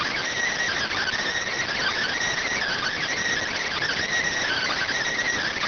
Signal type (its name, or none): Emergency Alert System (SAME) tones